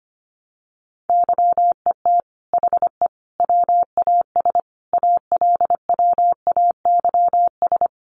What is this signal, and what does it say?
Morse code, heard as a keyed tone, about 25 wpm, 700 Hz. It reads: YET HE WAS ALWAYS
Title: Sentences from Top 1300 Words 25wpm